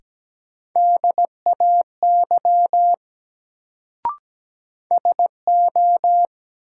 Morse code, heard as a keyed tone, about 17 wpm, 700 Hz. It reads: DAY SO
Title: Sets of 2 Words - New Word Drill - 17wpm